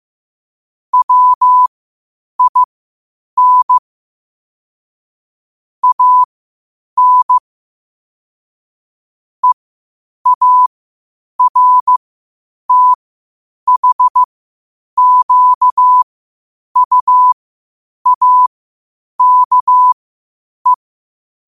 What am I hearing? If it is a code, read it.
Morse code, heard as a keyed tone, about 15 wpm, spaced as for 8 wpm, 1000 Hz. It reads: WIN AN EARTHQUAKE
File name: qotdfiles / 8wpm